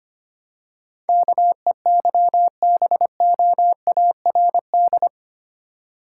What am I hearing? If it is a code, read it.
Morse code, heard as a keyed tone, about 28 wpm, 700 Hz. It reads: KEYBOARD
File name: Easy Words - Speed-Racing with Hints - 25wpm